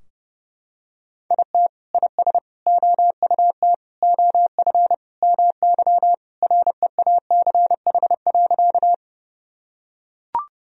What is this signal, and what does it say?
Morse code, heard as a keyed tone, about 30 wpm, 700 Hz. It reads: IT IS OUT OF MY REACH.
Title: Sentences from Top 1000 Words - Review Part 2 - 30wpm